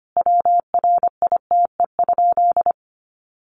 Morse code, heard as a keyed tone, about 25 wpm, 700 Hz. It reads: WRITE?
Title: Sentences from Top 800 Words 25wpm